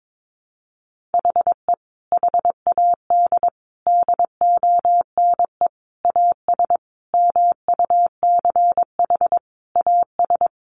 Morse code, heard as a keyed tone, about 22 wpm, 700 Hz. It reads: HE HAD DONE AS MUCH AS
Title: Sentences from Top 1200 Words 22wpm